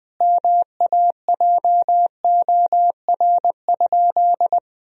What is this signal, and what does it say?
Morse code, heard as a keyed tone, about 21 wpm, 700 Hz. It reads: MAJOR?
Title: Sentences from Top 1000 Words - Review Part 4 - 20wpm